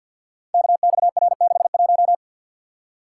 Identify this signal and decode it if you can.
Morse code, heard as a keyed tone, about 52 wpm, 700 Hz. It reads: KXR61
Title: Code Groups - 5 Letter:Numbers - 50wpm